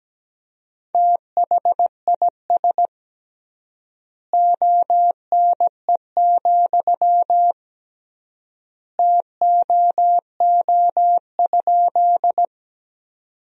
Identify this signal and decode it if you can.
Morse code, heard as a keyed tone, about 17 wpm, 700 Hz. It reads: THIS ONE, TOO?
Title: 3x Word-Spacing - Sentences from Top 200 Words - 17wpm